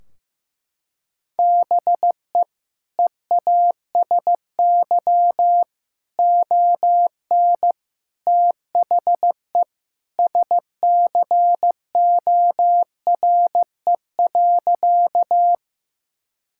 Morse code, heard as a keyed tone, about 15 wpm, 700 Hz. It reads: BE EASY ON THE SCORE.
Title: Sentences from Top 1600 Words 15wpm